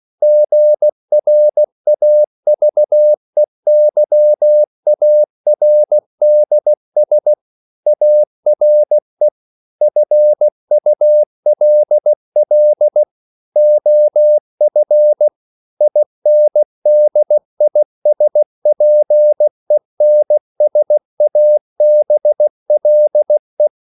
Morse code, heard as a keyed tone, about 16 wpm, 600 Hz. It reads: GRAVEYARDS ARE FULL OF INDISPENSABLE